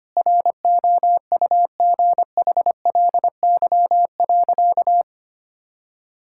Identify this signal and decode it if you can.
Morse code, heard as a keyed tone, about 25 wpm, 700 Hz. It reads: ROUGHLY.
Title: Sentences from Top 1600 Words 25wpm